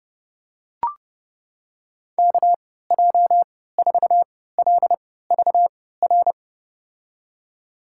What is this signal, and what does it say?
Morse code, heard as a keyed tone, about 30 wpm, spaced as for 17 wpm, 700 Hz. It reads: KJ4LVR
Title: USA Call Signs - 6 Characters - 3x Character Spacing - 30wpm